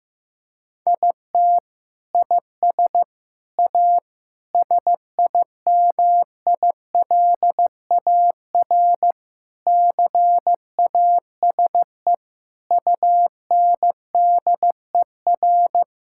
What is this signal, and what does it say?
Morse code, heard as a keyed tone, about 15 wpm, 700 Hz. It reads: IT IS A SIMILAR CASE UNDER
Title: Sentences from Top 1000 Words - Review Part 5 - 15wpm